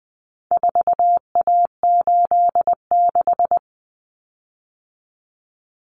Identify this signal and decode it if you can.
Morse code, heard as a keyed tone, about 20 wpm, 700 Hz. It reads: 4A86
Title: Common Cut Numbers - 4-Digit 20wpm